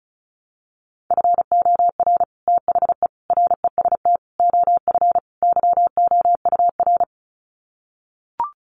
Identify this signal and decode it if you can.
Morse code, heard as a keyed tone, about 34 wpm, 700 Hz. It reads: FOR THE REST OF YOUR
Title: Sets of 5 Words Encore 35wpm